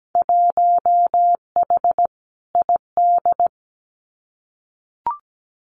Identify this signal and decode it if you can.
Morse code, heard as a keyed tone, about 17 wpm, 700 Hz. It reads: 1H ID
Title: Winter Field Day Exchanges 17wpm